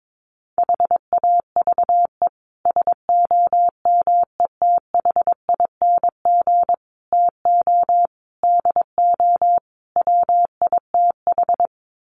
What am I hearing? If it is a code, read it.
Morse code, heard as a keyed tone, about 22 wpm, 700 Hz. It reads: HAVE SOMETHING TO DO WITH
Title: Sets of 5 Words Encore 22wpm